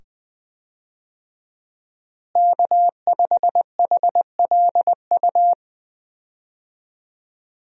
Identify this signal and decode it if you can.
Morse code, heard as a keyed tone, about 20 wpm, 700 Hz. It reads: K5HLU
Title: USA Call Signs - 5 Characters - Repeated 2x - 20wpm